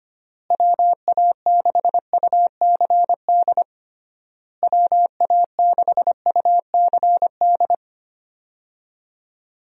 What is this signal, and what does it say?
Morse code, heard as a keyed tone, about 25 wpm, 700 Hz. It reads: WA6UCD WA6UCD
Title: USA Call Signs - 6 Characters - Repeated 3x - 25wpm